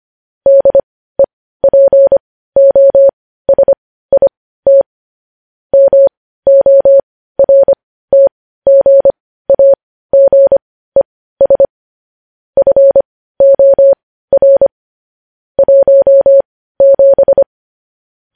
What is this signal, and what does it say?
Morse code, heard as a keyed tone, about 25 wpm, spaced as for 15 wpm, 550 Hz. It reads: DEPOSIT MORTGAGES FOR 17
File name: podcast / News headlines-1x-25x15